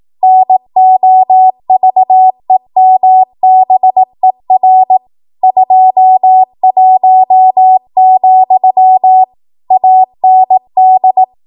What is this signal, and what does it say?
Morse code, heard as a keyed tone, about 18 wpm, 750 Hz. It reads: NOVEMBER 21, AND
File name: Archive / 18 WPM / 18WPM